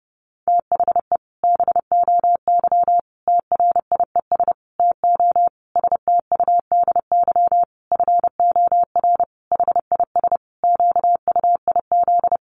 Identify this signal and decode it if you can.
Morse code, heard as a keyed tone, about 30 wpm, 700 Hz. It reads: THE BOY TRIES TO STUDY FOR HIS QUIZ
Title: Additional Easy Phrases 30wpm